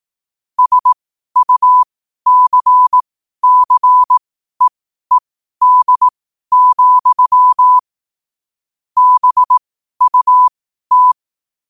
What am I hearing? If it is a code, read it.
Morse code, heard as a keyed tone, about 18 wpm, spaced as for 12 wpm, 1000 Hz. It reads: SUCCEED, BUT
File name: qotdfiles / 12wpm